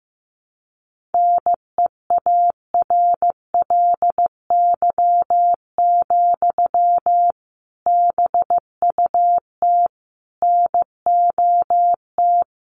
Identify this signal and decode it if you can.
Morse code, heard as a keyed tone, about 15 wpm, 700 Hz. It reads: NEARLY, BUT NOT
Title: Sentences from Top 1300 Words 15wpm